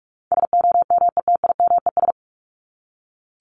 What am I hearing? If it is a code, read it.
Morse code, heard as a keyed tone, about 44 wpm, 700 Hz. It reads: SOMETIMES
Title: ES:S Suffix - Top 100 Words with Base Words 45wpm